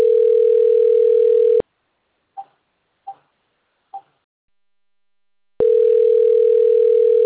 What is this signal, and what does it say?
Ringback tone (North America), 440 + 480 Hz.
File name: out- - - - - .19832